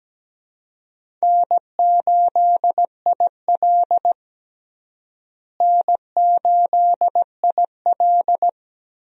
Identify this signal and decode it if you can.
Morse code, heard as a keyed tone, about 17 wpm, 700 Hz. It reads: N8IL N8IL
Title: USA Call Signs - 4 Characters - Repeated 2x - 17wpm